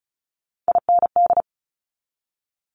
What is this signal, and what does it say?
Morse code, heard as a keyed tone, about 36 wpm, 700 Hz. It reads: IND
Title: Multi-letter Phonemes 35wpm